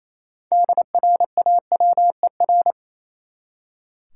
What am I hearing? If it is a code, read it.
Morse code, heard as a keyed tone, about 28 wpm, 700 Hz. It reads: DRAWER
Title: Six Letter Words - 28wpm